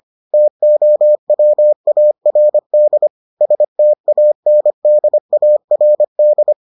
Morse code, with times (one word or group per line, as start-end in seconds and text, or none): 0.3-3.1 TOWARD
3.4-6.5 STANDARD